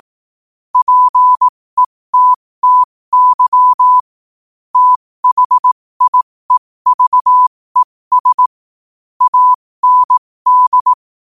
Morse code, heard as a keyed tone, about 18 wpm, spaced as for 15 wpm, 1000 Hz. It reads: PETTY THIEVES AND
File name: qotdfiles / 15wpm